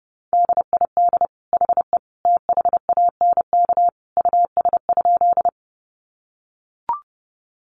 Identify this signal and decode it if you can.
Morse code, heard as a keyed tone, about 30 wpm, 700 Hz. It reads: DID HE THANK US?
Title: Sentences from Top 600 Words 30wpm